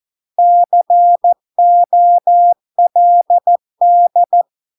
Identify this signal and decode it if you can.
Morse code, heard as a keyed tone, about 14 wpm, 700 Hz. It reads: COLD